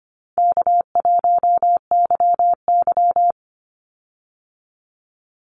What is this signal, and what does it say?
Morse code, heard as a keyed tone, about 25 wpm, 700 Hz. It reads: K1YY
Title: Easy Call Signs Repeated 2x - 25wpm